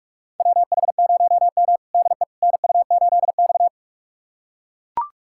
Morse code, heard as a keyed tone, about 41 wpm, 700 Hz. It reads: WH0M DE NU8X
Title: USA Callsign DE Callsign - 4 Characters - 45wpm